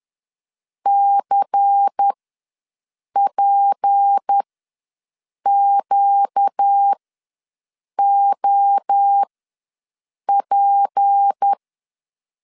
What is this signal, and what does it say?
Morse code, heard as a keyed tone, about 10 wpm, spaced as for 6 wpm, 800 Hz. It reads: CPQOP
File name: LEZIONI / 6 PKX0 / 6 P ws 10wpm 3751